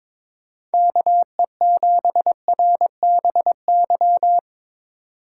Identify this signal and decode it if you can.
Morse code, heard as a keyed tone, about 22 wpm, 700 Hz. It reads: KE7RBY
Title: USA Call Signs - 6 Characters - 2x Character Spacing - 22wpm